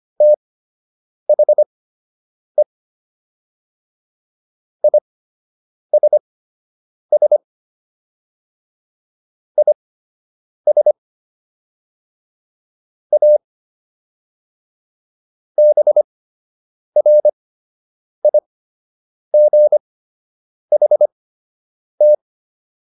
Morse code, heard as a keyed tone, about 25 wpm, spaced as for 8 wpm, 600 Hz. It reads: THE ISS IS A BRIGHT